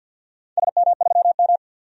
Morse code, heard as a keyed tone, about 47 wpm, 700 Hz. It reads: SM3M
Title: DXCC-100 Call Signs - 4 Characters - 50wpm